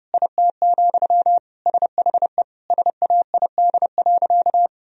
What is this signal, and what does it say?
Morse code, heard as a keyed tone, about 30 wpm, 700 Hz. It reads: IT, SHE SAID.
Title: Sentences from Top 1000 Words - Review Part 4 - 30wpm